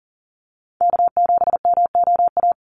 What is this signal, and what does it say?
Morse code, heard as a keyed tone, about 40 wpm, 700 Hz. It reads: K7MOA
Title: Call Signs Repeated 2x - 40wpm